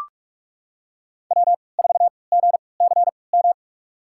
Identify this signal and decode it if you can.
Morse code, heard as a keyed tone, about 42 wpm, spaced as for 25 wpm, 700 Hz. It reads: W4GCM
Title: USA Call Signs - 5 Characters - 3x Character Spacing - 45wpm